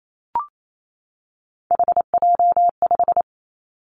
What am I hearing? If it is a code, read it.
Morse code, heard as a keyed tone, about 28 wpm, 700 Hz. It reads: HJ5
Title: Callsign Prefixes for Top 100 DXCC - 28wpm